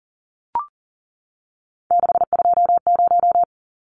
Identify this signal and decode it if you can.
Morse code, heard as a keyed tone, about 40 wpm, 700 Hz. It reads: 620
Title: Numbers - 3-Digit 40wpm